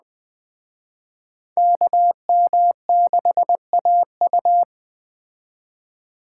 Morse code, heard as a keyed tone, about 21 wpm, 700 Hz. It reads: KM6AU
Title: Speed Racing - USA 2x2 Call Signs 17wpm